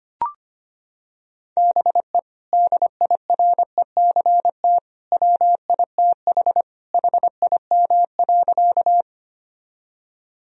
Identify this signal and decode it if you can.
Morse code, heard as a keyed tone, about 25 wpm, 700 Hz. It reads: BE DIRECT WITH HIM.
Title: Sentences from Top 1000 Words - Review Part 2 - 25wpm